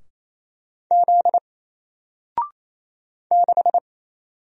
Morse code, heard as a keyed tone, about 28 wpm, 700 Hz. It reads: Z 6
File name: Single-Letter-Number-28wpm